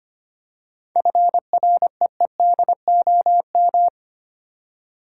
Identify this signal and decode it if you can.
Morse code, heard as a keyed tone, about 25 wpm, 700 Hz. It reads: FREEDOM
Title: Seven Letter Words - 25wpm